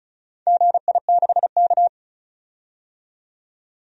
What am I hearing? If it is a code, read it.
Morse code, heard as a keyed tone, about 36 wpm, 690 Hz. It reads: GI6K
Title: DXCC-100 Call Signs - 4 Characters - 35wpm